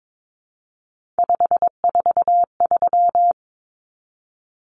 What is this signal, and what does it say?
Morse code, heard as a keyed tone, about 22 wpm, 700 Hz. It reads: 543
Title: Numbers - 3-Digit 22wpm